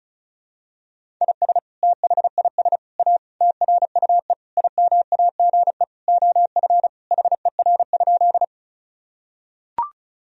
Morse code, heard as a keyed tone, about 36 wpm, 700 Hz. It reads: IS THIS A TRUE IMAGE OF HER?
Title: Sentences from Top 1000 Words - Review Part 4 - 35wpm